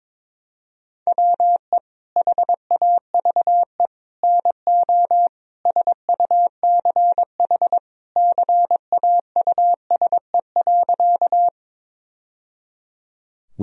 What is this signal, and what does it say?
Morse code, heard as a keyed tone, about 22 wpm, 700 Hz. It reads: WE HAVE NO SUCH CAUSE.
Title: Sentences from Top 1000 Words - Review Part 1 - 22wpm